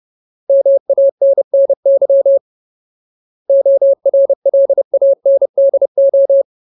Morse code, heard as a keyed tone, about 30 wpm, 550 Hz. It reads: MANNY ORLANDO
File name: Ultimate Word List / 30WPM Ultimate